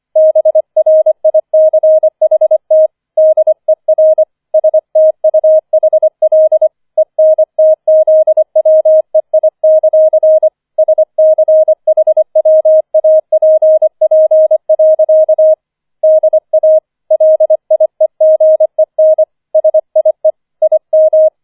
Morse code, heard as a keyed tone, about 25 wpm, 630 Hz. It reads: BRICHT DER STUHL ENTZWEI; SCHWAPP. DA LIEGEN SIE IM
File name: 25wpm